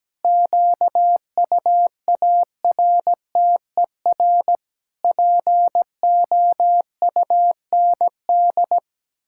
Morse code, heard as a keyed tone, about 17 wpm, 700 Hz. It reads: QUARTER POUND
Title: Pangram Challenge v2 - 17wpm